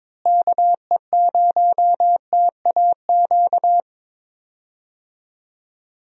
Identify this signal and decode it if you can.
Morse code, heard as a keyed tone, about 22 wpm, 700 Hz. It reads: KE0TAQ
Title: USA Call Signs - 6 Characters - Repeated 2x - 22wpm